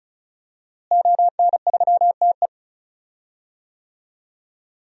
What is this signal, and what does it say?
Morse code, heard as a keyed tone, about 36 wpm, 700 Hz. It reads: ON3TE
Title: DXCC-100 Call Signs - 5 Characters - Repeated 2x - 35wpm